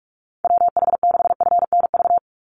Morse code, heard as a keyed tone, about 44 wpm, 700 Hz. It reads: WH6FNV
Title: USA Call Signs 45wpm